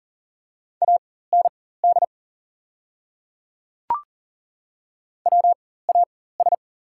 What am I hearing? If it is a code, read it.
Morse code, heard as a keyed tone, about 40 wpm, spaced as for 19 wpm, 690 Hz. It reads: AND WAS
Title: 4x Character Spacing - Top 100 Words - 40wpm